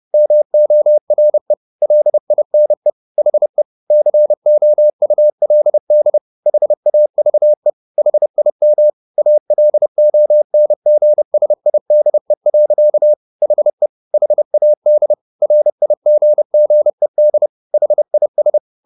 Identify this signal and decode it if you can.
Morse code, heard as a keyed tone, about 30 wpm, 600 Hz. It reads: MORE LINE HE COULD HAVE HIM ALONGSIDE. HE HAD RIGGED HIS